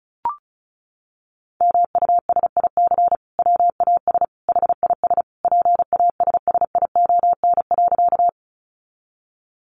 Morse code, heard as a keyed tone, about 34 wpm, 700 Hz. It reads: MUSIC WAS HIS PASSION.
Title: Sentences from Top 1300 Words 35wpm